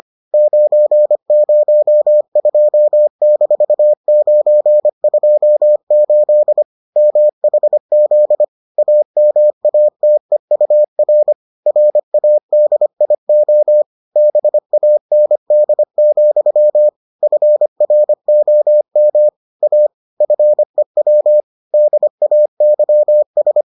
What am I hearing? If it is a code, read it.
Morse code, heard as a keyed tone, about 25 wpm, 600 Hz. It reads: 902-928 MHZ AMATEUR RADIO BAND, FROM A FEW DAYS